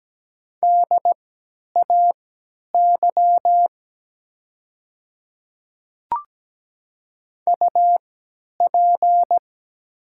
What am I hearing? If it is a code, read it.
Morse code, heard as a keyed tone, about 17 wpm, spaced as for 10 wpm, 700 Hz. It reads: DAY UP
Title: 3x Character Spacing - Top 100 Words - 17wpm